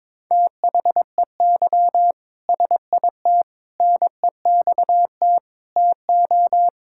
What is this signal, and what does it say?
Morse code, heard as a keyed tone, about 22 wpm, 700 Hz. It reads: THEY SIT NEXT TO
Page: morse-code-ninja-practice-22wpm